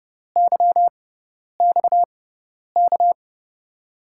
Morse code, heard as a keyed tone, about 31 wpm, spaced as for 10 wpm, 700 Hz. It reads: YXK
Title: USA Call Signs - 6 Characters - 6x Character Spacing - 30wpm